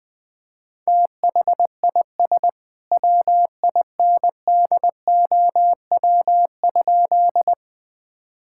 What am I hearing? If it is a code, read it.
Morse code, heard as a keyed tone, about 20 wpm, 700 Hz. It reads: THIS WINDOW?
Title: Sentences from Top 1600 Words 20wpm